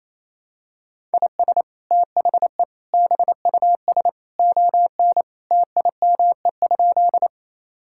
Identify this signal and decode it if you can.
Morse code, heard as a keyed tone, about 28 wpm, 700 Hz. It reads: IS THE BUS ON TIME?
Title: Sentences from Top 1000 Words - Review Part 6 - 28wpm